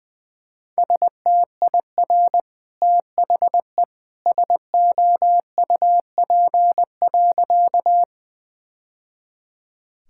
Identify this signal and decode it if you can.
Morse code, heard as a keyed tone, about 19 wpm, 700 Hz. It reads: STIR THE SOUP.
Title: Easy Sentences with Hints - 20wpm